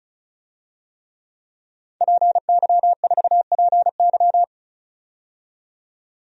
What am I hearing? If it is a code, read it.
Morse code, heard as a keyed tone, about 34 wpm, 700 Hz. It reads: PY4PY
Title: DXCC-100 Call Signs - 5 Characters - 35wpm